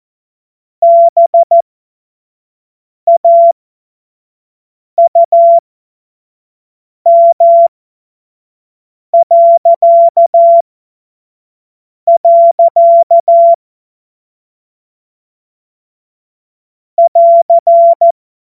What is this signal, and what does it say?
Morse code, heard as a keyed tone, about 14 wpm, spaced as for 5 wpm, 680 Hz. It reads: BAUM.. <AR>
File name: Morsekurs / Witze / Witz 16 Januar 2024 25 F70